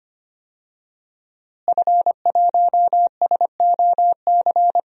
Morse code, heard as a keyed tone, about 25 wpm, 700 Hz. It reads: F1SOC
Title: DXCC-100 Call Signs - 5 Characters - Repeated 2x - 25wpm